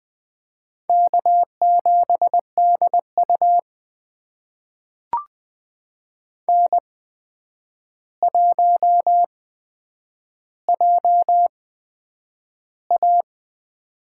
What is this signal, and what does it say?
Morse code, heard as a keyed tone, about 20 wpm, spaced as for 9 wpm, 700 Hz. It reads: K7DU N1JA
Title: USA Call Signs - 4 Characters - 8x Character Spacing - 20wpm